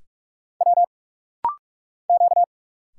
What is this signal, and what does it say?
Morse code, heard as a keyed tone, about 46 wpm, 700 Hz. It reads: W Q
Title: Single Letters - Rapid-Fire 45wpm